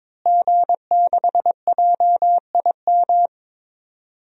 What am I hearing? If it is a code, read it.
Morse code, heard as a keyed tone, about 23 wpm, 700 Hz. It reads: G6JIM
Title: DXCC-100 Call Signs - 5 Characters - Repeated 2x - 22wpm